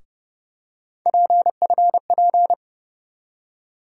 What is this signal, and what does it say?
Morse code, heard as a keyed tone, about 30 wpm, 700 Hz. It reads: PFP
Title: USA Call Sign Postfixes - 3 Letters 30wpm